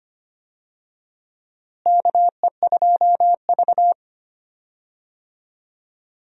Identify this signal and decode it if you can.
Morse code, heard as a keyed tone, about 25 wpm, 700 Hz. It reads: KE2V
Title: USA Call Signs - 4 Characters - Repeated 2x - 25wpm